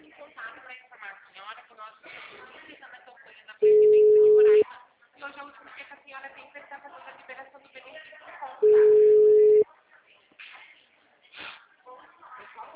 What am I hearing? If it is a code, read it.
Ringback tone (Europe, 425 Hz), with speech.